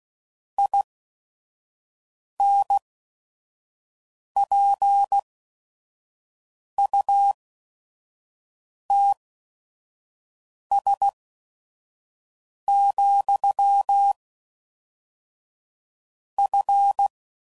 Morse code, heard as a keyed tone, about 16 wpm, spaced as for 5 wpm, 800 Hz. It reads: INPUTS, F
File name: CW Code Groups @5WPM / Technical Subjects / AND Gate